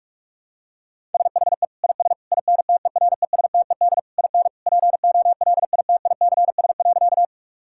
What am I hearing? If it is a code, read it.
Morse code, heard as a keyed tone, about 41 wpm, 700 Hz. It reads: SHE IS INTERESTED IN POLITICS.
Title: Sentences from Top 1000 Words - Review Part 6 - 45wpm